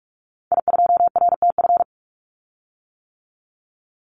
Morse code, heard as a keyed tone, about 44 wpm, 700 Hz. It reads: I2RTF
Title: Call Signs - 45wpm